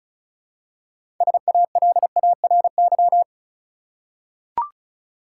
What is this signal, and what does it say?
Morse code, heard as a keyed tone, about 34 wpm, 700 Hz. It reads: SALARY
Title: Six Letter Words - 35wpm